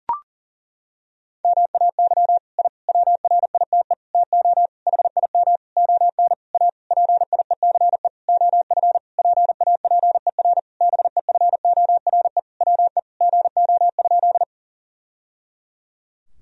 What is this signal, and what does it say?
Morse code, heard as a keyed tone, about 40 wpm, 700 Hz. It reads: MAY I WRITE TO HIM ON A PIECE OF PAPER BEFORE WE GO?
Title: Sentences from Top 1000 Words - Review Part 2 - 40wpm